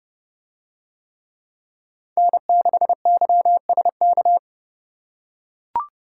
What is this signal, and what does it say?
Morse code, heard as a keyed tone, about 29 wpm, 700 Hz. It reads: N6YSK